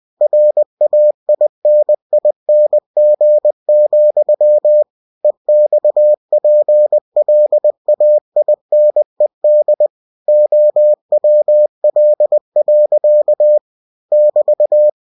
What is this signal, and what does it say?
Morse code, heard as a keyed tone, about 20 wpm, 600 Hz. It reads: RAINING, EXPLAINED OWL. <BT>
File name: cw-books / winnie-the-pooh / winnie-20wpm-page-0096